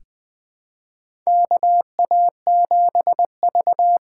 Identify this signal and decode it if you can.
Morse code, heard as a keyed tone, about 20 wpm, 700 Hz. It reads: KA7V
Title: USA Call Signs - 8x Character Spacing - 20wpm